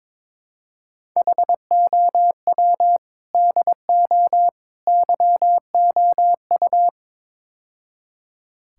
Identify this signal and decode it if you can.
Morse code, heard as a keyed tone, about 22 wpm, 700 Hz. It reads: HOW DO YOU
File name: practice / 22wpm / Build A Phrase - 5-Word Phrases - 22wpm